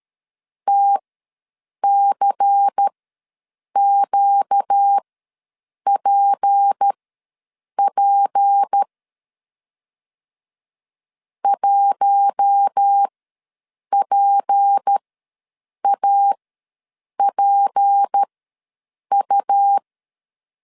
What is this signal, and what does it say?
Morse code, heard as a keyed tone, about 13 wpm, spaced as for 7 wpm, 800 Hz. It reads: TCQPP 1PAPU
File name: LEZIONI / 6 PKX0 / 6 P ws 12wpm 3117